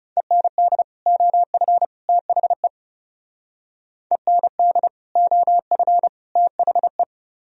Morse code, heard as a keyed tone, about 37 wpm, 700 Hz. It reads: END OF THE END OF THE
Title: Speed-Racing - Sets of 3 Words 30wpm